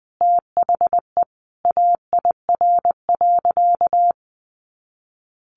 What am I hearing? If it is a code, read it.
Morse code, heard as a keyed tone, about 20 wpm, 700 Hz. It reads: THE AIR.
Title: Easy Phrases 20wpm